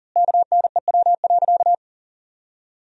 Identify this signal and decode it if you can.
Morse code, heard as a keyed tone, about 40 wpm, 700 Hz. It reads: KNEW.
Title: 4x Word-Spacing - Sentences from Top 400 Words - 40wpm